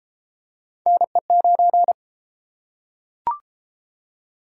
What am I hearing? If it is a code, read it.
Morse code, heard as a keyed tone, about 32 wpm, 700 Hz. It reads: NE9